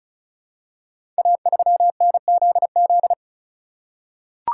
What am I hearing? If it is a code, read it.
Morse code, heard as a keyed tone, about 34 wpm, 700 Hz. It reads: A3NZZ
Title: Code Groups - 5 Letter:Numbers - 35wpm